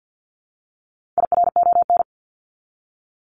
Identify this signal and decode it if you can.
Morse code, heard as a keyed tone, about 48 wpm, 700 Hz. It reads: IRON
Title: Top 1751-2000 Words 50wpm